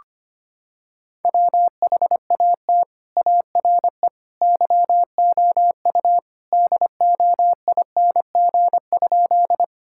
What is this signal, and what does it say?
Morse code, heard as a keyed tone, about 25 wpm, 700 Hz. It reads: WHAT ARE YOU DOING?
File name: practice / 25wpm / Sentences with 3-Word Phrases - 25wpm